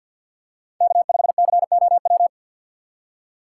Morse code, heard as a keyed tone, about 51 wpm, 700 Hz. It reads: K5COW
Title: USA Call Signs - 2x Character Spacing - 50wpm